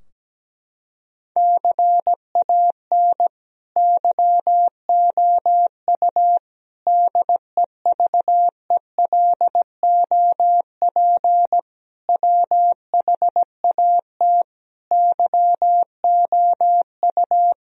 Morse code, heard as a keyed tone, about 17 wpm, 700 Hz. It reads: CAN YOU DEVELOP WHAT YOU
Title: Sentences from Top 1000 Words - Review Part 2 - 17wpm